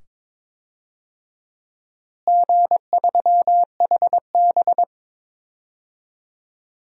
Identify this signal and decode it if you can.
Morse code, heard as a keyed tone, about 22 wpm, 700 Hz. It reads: G3HB
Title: DXCC-100 Call Signs - 4 Characters - Repeated 2x - 22wpm